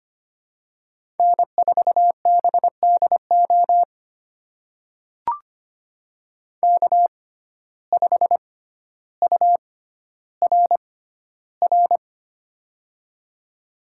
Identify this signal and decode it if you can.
Morse code, heard as a keyed tone, about 25 wpm, spaced as for 13 wpm, 700 Hz. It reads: N4BDO K5URR
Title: USA Call Signs - 5 Characters - 6x Character Spacing - 25wpm